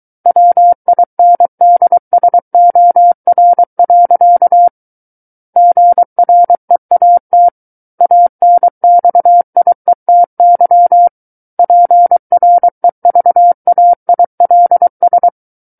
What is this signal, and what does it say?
Morse code, heard as a keyed tone, about 23 wpm, 700 Hz. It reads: WINDSOR. GREAT ANXIETY PREVAILS
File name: Book One / 14-Chapter 14 - 23 WPM